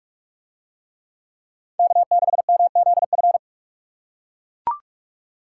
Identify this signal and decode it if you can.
Morse code, heard as a keyed tone, about 40 wpm, 700 Hz. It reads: K6MZF